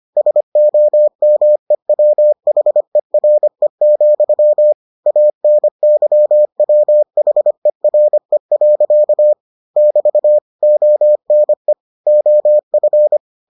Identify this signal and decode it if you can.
Morse code, heard as a keyed tone, about 25 wpm, 600 Hz. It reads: SOMEWHERE, ANYWHERE. <BT> ONE OF